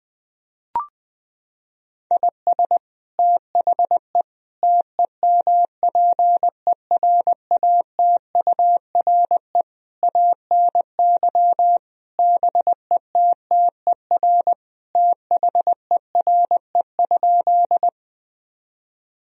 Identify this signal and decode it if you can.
Morse code, heard as a keyed tone, about 20 wpm, 700 Hz. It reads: IS THE TEMPERATURE ANY BETTER THERE?
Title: Sentences from Top 1000 Words - Review Part 4 - 20wpm